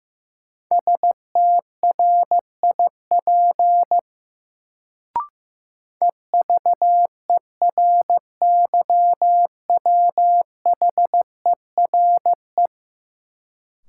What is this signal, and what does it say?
Morse code, heard as a keyed tone, about 15 wpm, 700 Hz. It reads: STRIP EVERYWHERE
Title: Sentences from Top 1600 Words - New Word Drill - 15wpm